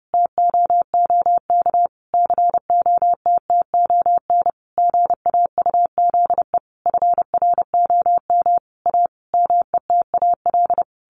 Morse code, heard as a keyed tone, about 30 wpm, 700 Hz. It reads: TOOK COTTON GAUZE FROM A METAL
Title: A Sound Of Thunder 30wpm